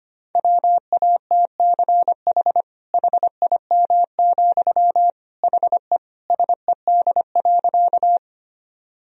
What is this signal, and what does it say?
Morse code, heard as a keyed tone, about 25 wpm, 700 Hz. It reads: WATCH HIM, HE SED.
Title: Sentences from Top 300 Words and Abbreviations 25wpm